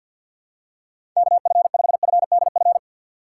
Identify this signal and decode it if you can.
Morse code, heard as a keyed tone, about 46 wpm, 700 Hz. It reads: KF5FDF
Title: USA Call Signs - 6 Characters - 4x Character Spacing - 50wpm